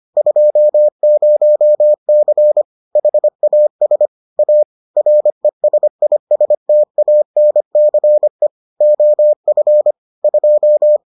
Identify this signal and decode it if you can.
Morse code, heard as a keyed tone, about 25 wpm, 600 Hz. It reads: 20C HAS A RESISTANCE OF 2